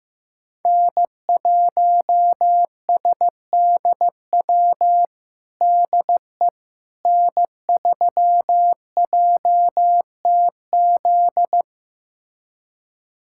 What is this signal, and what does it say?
Morse code, heard as a keyed tone, about 16 wpm, 700 Hz. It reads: N1SDW DE N3JTZ
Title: USA Callsign DE Callsign - 5 Characters - 15wpm